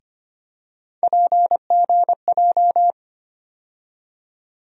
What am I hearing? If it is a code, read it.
Morse code, heard as a keyed tone, about 25 wpm, 700 Hz. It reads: PGJ
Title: USA Call Sign Postfixes - 3 Letters 25wpm